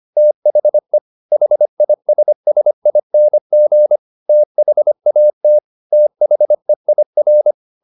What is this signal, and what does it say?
Morse code, heard as a keyed tone, about 25 wpm, 600 Hz. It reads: THE HISSING THAT THEIR
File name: the old man and the sea part 1 25wpm